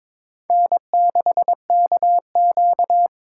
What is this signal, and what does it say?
Morse code, heard as a keyed tone, about 21 wpm, 700 Hz. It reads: N6KQ
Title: USA Call Signs - 4 Characters - 2x Character Spacing - 22wpm